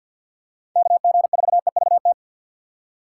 Morse code, heard as a keyed tone, about 47 wpm, 700 Hz. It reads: KG4EVT
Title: USA Call Signs - 6 Characters - 8x Character Spacing - 50wpm